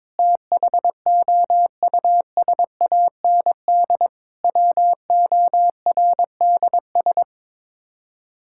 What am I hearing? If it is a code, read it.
Morse code, heard as a keyed tone, about 22 wpm, 700 Hz. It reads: THOUSAND WORDS
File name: practice / 22wpm / Sets of 7 Words - 22wpm